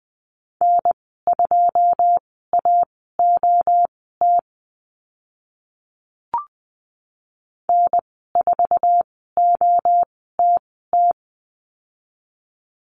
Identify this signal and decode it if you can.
Morse code, heard as a keyed tone, about 20 wpm, spaced as for 14 wpm, 700 Hz. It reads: N2AOT N4OTT
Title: 082 Lesson 14g - Number 2 - Call Signs 20wpm